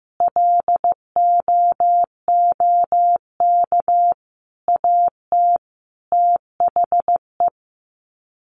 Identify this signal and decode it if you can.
Morse code, heard as a keyed tone, about 15 wpm, 700 Hz. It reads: LOOK AT THE
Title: Sets of 5 Words 15wpm